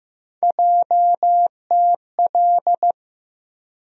Morse code, heard as a keyed tone, about 15 wpm, 700 Hz. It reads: JTL
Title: USA Call Sign Postfixes - 3 Letters 15wpm